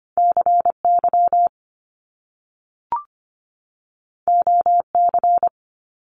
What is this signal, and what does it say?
Morse code, heard as a keyed tone, about 25 wpm, 700 Hz. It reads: CY OC
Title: Top Two Letter Combinations in Words 25wpm